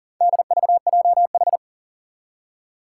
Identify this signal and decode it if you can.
Morse code, heard as a keyed tone, about 40 wpm, 700 Hz. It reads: DVJH
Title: Four Challenging Letter-Number Combinations 40wpm